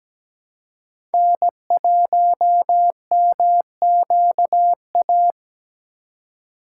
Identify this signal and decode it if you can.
Morse code, heard as a keyed tone, about 20 wpm, 700 Hz. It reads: N1MQA
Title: Speed Racing - USA 1x3 Call Signs 17wpm